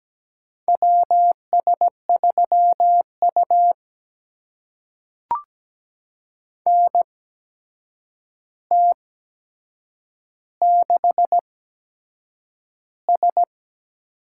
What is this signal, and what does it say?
Morse code, heard as a keyed tone, about 17 wpm, spaced as for 7 wpm, 700 Hz. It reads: WS3U NT6S
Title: USA Call Signs - 4 Characters - 8x Character Spacing - 17wpm